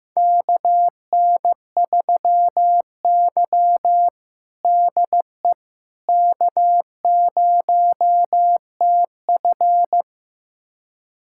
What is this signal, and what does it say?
Morse code, heard as a keyed tone, about 15 wpm, 700 Hz. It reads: KN3Y DE K0TF
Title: USA Callsign DE Callsign - 4 Characters - 15wpm